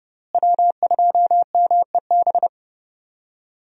Morse code, heard as a keyed tone, about 30 wpm, 700 Hz. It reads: W2MEB
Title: USA Call Signs - 5 Characters - Repeated 2x - 30wpm